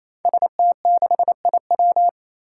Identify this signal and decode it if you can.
Morse code, heard as a keyed tone, about 28 wpm, 700 Hz. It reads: ST6IW
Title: Five Challenging Letter-Number Combinations 28wpm